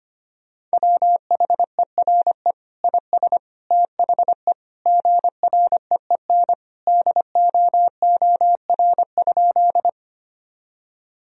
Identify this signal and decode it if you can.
Morse code, heard as a keyed tone, about 25 wpm, 700 Hz. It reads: WHERE IS THE GREEN DOOR?
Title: Sentences from Top 1000 Words - Review Part 2 - 25wpm